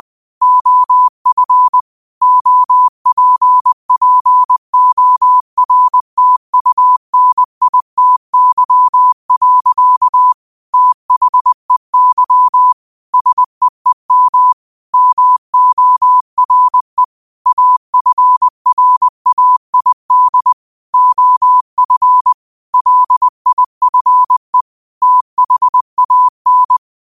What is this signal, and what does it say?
Morse code, heard as a keyed tone, about 20 wpm, 1000 Hz. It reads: OF OPPORTUNITY. THEY SEEM MORE AFRAID OF LIFE THAN
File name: qotdfiles / 20wpm